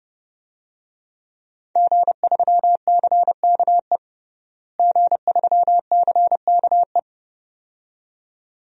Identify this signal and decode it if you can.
Morse code, heard as a keyed tone, about 30 wpm, 700 Hz. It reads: G3CKE G3CKE
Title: DXCC-100 Call Signs - 5 Characters - Repeated 2x - 30wpm